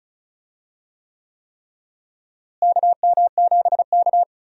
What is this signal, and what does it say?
Morse code, heard as a keyed tone, about 36 wpm, 700 Hz. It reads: KM7K
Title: USA Call Signs - 4 Characters - Repeated 3x - 35wpm